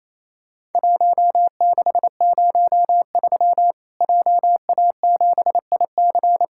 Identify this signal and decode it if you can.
Morse code, heard as a keyed tone, about 28 wpm, 700 Hz. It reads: 1603 JA7IC
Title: Worked All Europe DX Contest - QTC Exchange - 28wpm